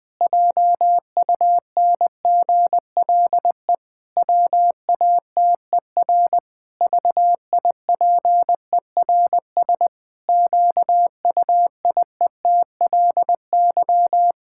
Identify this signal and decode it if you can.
Morse code, heard as a keyed tone, about 21 wpm, 700 Hz. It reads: JUNGLE WATER VIPERS QUIETLY
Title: Pangram Challenge v2 - 20wpm